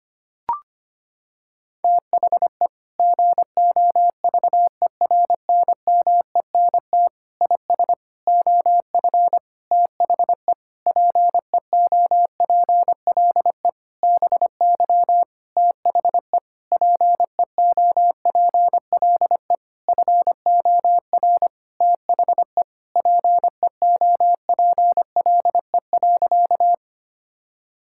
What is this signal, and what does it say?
Morse code, heard as a keyed tone, about 25 wpm, 700 Hz. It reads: THE GOVERNMENT IS OF THE PEOPLE BY THE PEOPLE FOR THE PEOPLE.
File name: practice / 25wpm / Sentences with 9-Word Phrases - 25wpm